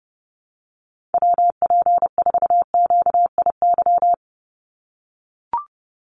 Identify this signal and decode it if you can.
Morse code, heard as a keyed tone, about 30 wpm, 700 Hz. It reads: WP4QIY
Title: USA Call Signs 30wpm